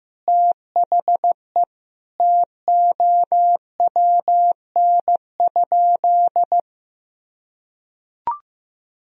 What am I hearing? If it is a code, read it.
Morse code, heard as a keyed tone, about 15 wpm, 700 Hz. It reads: THE TOWN?
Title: Sentences from Top 1000 Words - Review Part 2 - 15wpm